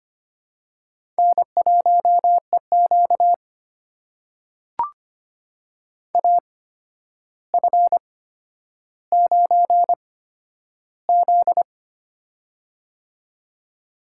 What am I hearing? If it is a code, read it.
Morse code, heard as a keyed tone, about 25 wpm, spaced as for 11 wpm, 700 Hz. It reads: N1EQ AF9Z
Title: USA Call Signs - 4 Characters - 8x Character Spacing - 25wpm